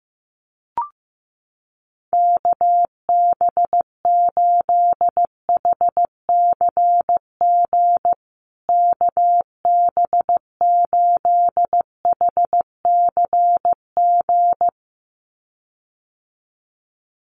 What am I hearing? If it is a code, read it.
Morse code, heard as a keyed tone, about 15 wpm, 700 Hz. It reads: KB8HCG KB8HCG
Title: USA Call Signs Repeated 2x 15wpm